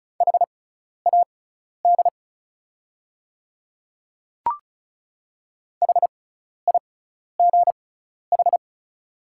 Morse code, heard as a keyed tone, about 36 wpm, spaced as for 12 wpm, 690 Hz. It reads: HAD HIGH
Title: 6x Character Spacing - Top 300 Words - 35wpm